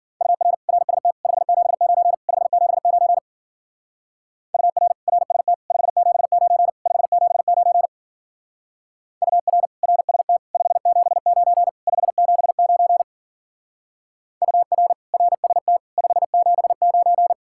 Morse code, heard as a keyed tone, about 50 wpm, 700 Hz. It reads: UR RST 579 579 UR RST 579 579 UR RST 579 579 UR RST 579
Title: Speed-Racing Common QSO Phrases - 40wpm